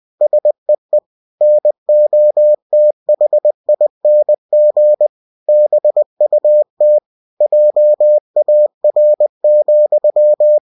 Morse code, heard as a keyed tone, about 20 wpm, 600 Hz. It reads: SEE NOTHING BUT JAR,